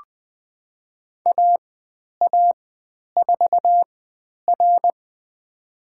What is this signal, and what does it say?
Morse code, heard as a keyed tone, about 20 wpm, spaced as for 10 wpm, 700 Hz. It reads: AA4R
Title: 095 Lesson 16g - Letter c - Call Signs 20wpm